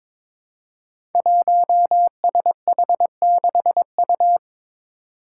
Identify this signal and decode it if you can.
Morse code, heard as a keyed tone, about 22 wpm, 700 Hz. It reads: 1SH6U
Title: Five Challenging Letter-Number Combinations 22wpm